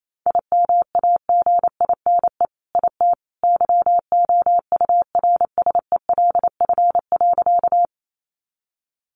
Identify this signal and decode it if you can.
Morse code, heard as a keyed tone, about 28 wpm, 700 Hz. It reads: IMAGINE IT YOURSELF.
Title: Sentences from Top 1400 Words 28wpm